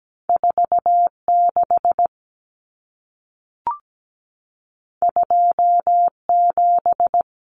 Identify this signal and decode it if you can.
Morse code, heard as a keyed tone, about 17 wpm, 700 Hz. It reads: 46 27
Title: Numbers - 2-Digit 17wpm